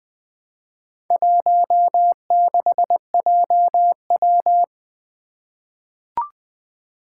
Morse code, heard as a keyed tone, about 20 wpm, 700 Hz. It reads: 16JW
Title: Four Challenging Letter-Number Combinations 20wpm